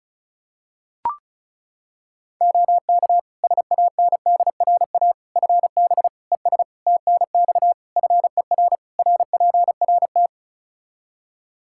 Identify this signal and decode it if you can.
Morse code, heard as a keyed tone, about 36 wpm, 700 Hz. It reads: OK SANDRA FB ES TNX FER RPRT
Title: Practice QSO - Exchange 2 - Info - 35wpm